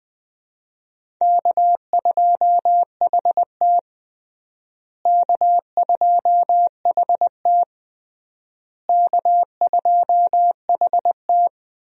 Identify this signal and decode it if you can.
Morse code, heard as a keyed tone, about 20 wpm, 700 Hz. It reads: K2HT K2HT K2HT
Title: USA Call Signs - 4 Characters - Repeated 3x - 20wpm